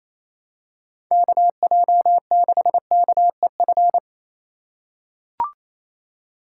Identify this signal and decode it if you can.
Morse code, heard as a keyed tone, about 28 wpm, 700 Hz. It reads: KJ6KEF
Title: USA Call Signs - 6 Characters - 3x Character Spacing - 28wpm